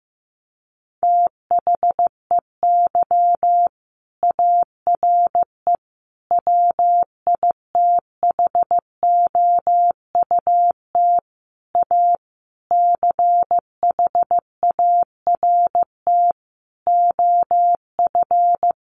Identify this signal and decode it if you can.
Morse code, heard as a keyed tone, about 15 wpm, 700 Hz. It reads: THEY ARE WITHOUT A CHART OF
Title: Sentences from Top 1300 Words 15wpm